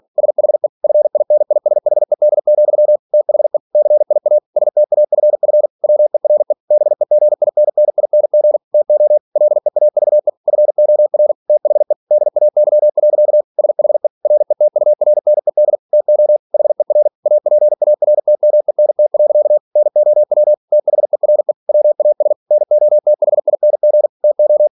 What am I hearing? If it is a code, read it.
Morse code, heard as a keyed tone, about 46 wpm, 600 Hz. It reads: SHE FINISHED, THE CIA STAFF WERE BEGINNING TO LEAVE FOR THE DAY. SHE RETURNED TO HER APARTMENT. NOW THERE WAS NOTHING TO
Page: Never 47wpm